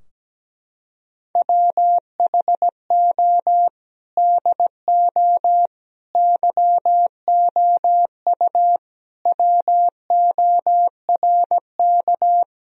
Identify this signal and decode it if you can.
Morse code, heard as a keyed tone, about 17 wpm, 700 Hz. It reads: WHO DO YOU WORK
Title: Sentences from Top 1000 Words - Review Part 4 - 17wpm